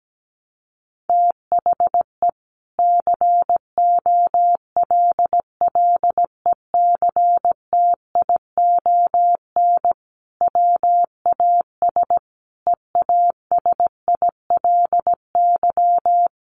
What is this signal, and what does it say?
Morse code, heard as a keyed tone, about 17 wpm, 700 Hz. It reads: THE COLLECTION WAS EASILY
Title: Sentences from Top 1200 Words 17wpm